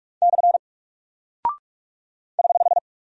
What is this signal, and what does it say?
Morse code, heard as a keyed tone, about 45 wpm, 700 Hz. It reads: / <HH>
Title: Single Character (Letter:Number:Punctuation:Prosign) Alt 45wpm